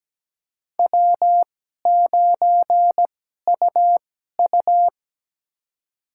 Morse code, heard as a keyed tone, about 17 wpm, spaced as for 12 wpm, 700 Hz. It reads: W9UU
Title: USA Call Signs - 4 Characters - 2x Character Spacing - 17wpm